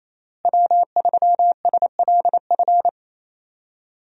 Morse code, heard as a keyed tone, about 28 wpm, 700 Hz. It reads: W3SLF
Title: USA Call Signs - 5 Characters - Repeated 3x - 28wpm